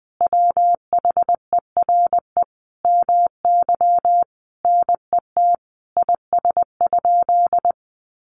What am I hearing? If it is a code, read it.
Morse code, heard as a keyed tone, about 20 wpm, 700 Hz. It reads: WHERE MY NET IS?
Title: Sentences from Top 1400 Words 20wpm